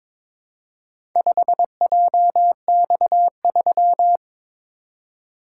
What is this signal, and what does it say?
Morse code, heard as a keyed tone, about 22 wpm, 700 Hz. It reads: 5JX3
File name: practice / 22wpm / Four Letter-Number Combinations - 22wpm